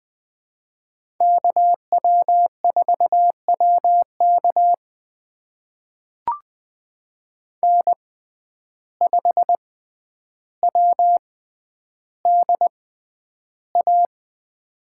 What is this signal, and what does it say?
Morse code, heard as a keyed tone, about 20 wpm, spaced as for 10 wpm, 700 Hz. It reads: KW4WK N5WDA
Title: USA Call Signs - 6x Character Spacing - 20wpm